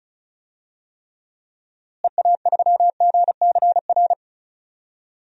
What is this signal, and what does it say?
Morse code, heard as a keyed tone, about 36 wpm, 700 Hz. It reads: EA3GCR
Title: DXCC-100 Call Signs - 6 Characters - Repeated 3x - 35wpm